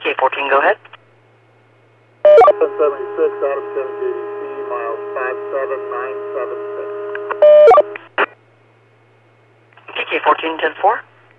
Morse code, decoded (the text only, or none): none